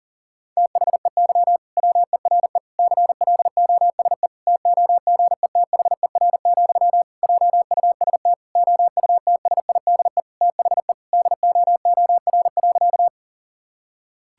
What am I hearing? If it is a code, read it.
Morse code, heard as a keyed tone, about 37 wpm, 700 Hz. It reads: THEY WERE CLOSE TOGETHER, JUST OUTSIDE THE DOOR.
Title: Sentences from Top 1000 Words - Review Part 6 - 40wpm